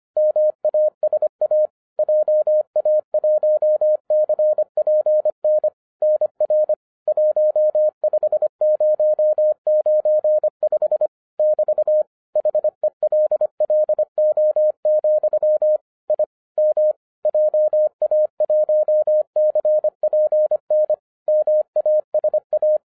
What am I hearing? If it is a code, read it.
Morse code, heard as a keyed tone, about 25 wpm, 610 Hz. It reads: MASA JA1CPN NR 15095 <BT> HELLO, I M JA1CPN MASA